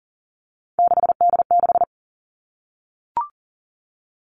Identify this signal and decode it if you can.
Morse code, heard as a keyed tone, about 41 wpm, 700 Hz. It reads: 6D6
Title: Three Challenging Letter-Number Combinations 40wpm